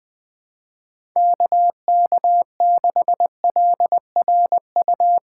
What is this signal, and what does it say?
Morse code, heard as a keyed tone, about 21 wpm, 700 Hz. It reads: KK6LRU
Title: USA Call Signs - 6 Characters - 4x Character Spacing - 20wpm